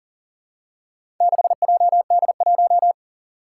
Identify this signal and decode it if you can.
Morse code, heard as a keyed tone, about 40 wpm, 700 Hz. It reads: 6JD1
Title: Four Challenging Letter-Number Combinations 40wpm